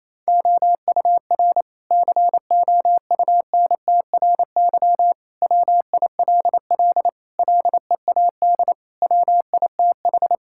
Morse code, heard as a keyed tone, about 27 wpm, 700 Hz. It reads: OUR COUNTRY WILL LEAD WITH
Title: Sentences from Top 1000 Words - Review Part 5 - 28wpm